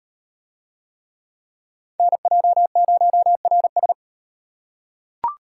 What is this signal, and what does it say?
Morse code, heard as a keyed tone, about 38 wpm, 700 Hz. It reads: NJ0RS